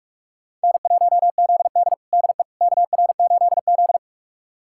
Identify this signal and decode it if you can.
Morse code, heard as a keyed tone, about 47 wpm, 700 Hz. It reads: N1ZD DE KR8Z
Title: USA Callsign DE Callsign - 4 Characters - 45wpm